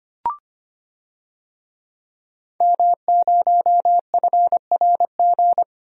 Morse code, heard as a keyed tone, about 25 wpm, 700 Hz. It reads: M0FRG
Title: DXCC-100 Call Signs - 5 Characters - Repeated 2x - 25wpm